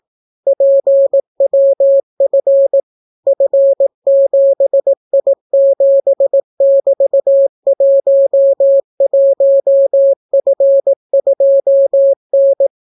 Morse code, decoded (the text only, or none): PWF F7I7<BT>11F2N